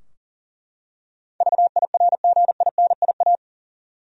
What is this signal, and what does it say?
Morse code, heard as a keyed tone, about 38 wpm, 700 Hz. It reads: VIRGINIA
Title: USA Common Names - 40wpm